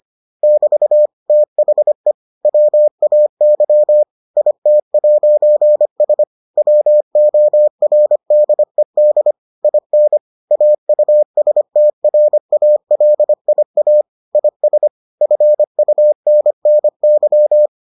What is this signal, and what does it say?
Morse code, heard as a keyed tone, about 25 wpm, 600 Hz. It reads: <BT> THE WAY IT'S WORDED IN AUSTRALIA IS FUNNY